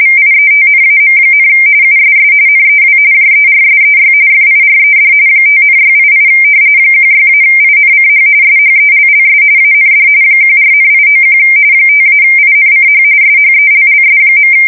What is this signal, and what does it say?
RTTY, about 45.45 baud. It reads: VIETNAM, USS NEW JERSEY FIRED 5,688 16-INCH SHELLS AND 14,891 5-INCH SHELLS.
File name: fsk170 / battleshipnj